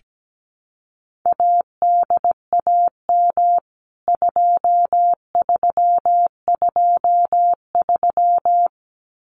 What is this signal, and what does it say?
Morse code, heard as a keyed tone, about 17 wpm, 700 Hz. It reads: ADAM 2323
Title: CWT Exchange 17wpm